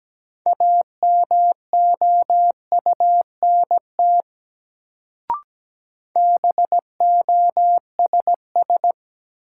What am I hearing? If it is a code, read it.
Morse code, heard as a keyed tone, about 17 wpm, 700 Hz. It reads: AMOUNT BOSS
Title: Sentences from Top 700 Words - New Word Drill - 17wpm